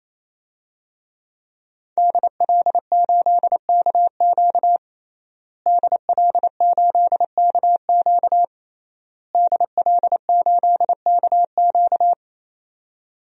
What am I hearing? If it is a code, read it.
Morse code, heard as a keyed tone, about 27 wpm, 700 Hz. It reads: DL8KQ DL8KQ DL8KQ
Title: DXCC-100 Call Signs - 5 Characters - Repeated 3x - 28wpm